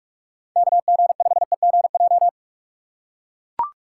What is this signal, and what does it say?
Morse code, heard as a keyed tone, about 43 wpm, 700 Hz. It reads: KG5EGJ
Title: USA Call Signs - 6 Characters - 3x Character Spacing - 45wpm